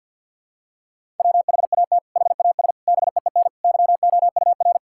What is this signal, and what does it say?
Morse code, heard as a keyed tone, about 52 wpm, 700 Hz. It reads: WHAT HAS BEEN YOUR
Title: Sentences from Top 1000 Words - Review Part 6 - 50wpm